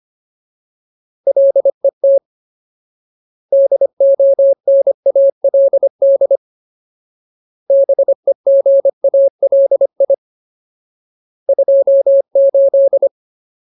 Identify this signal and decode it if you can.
Morse code, heard as a keyed tone, about 25 wpm, 550 Hz. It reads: LET DONALD BEGALI 28
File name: Ultimate Word List / 25WPM Ultimate